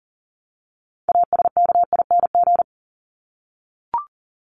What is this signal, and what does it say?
Morse code, heard as a keyed tone, about 40 wpm, 700 Hz. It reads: ASKING
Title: ING Suffix - Top 100 Words with Base Words 40wpm